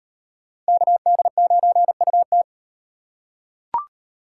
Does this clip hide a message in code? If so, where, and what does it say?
Yes: Morse code from 0.7 to 2.4 s, heard as a keyed tone, about 38 wpm, 700 Hz. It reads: KD9UT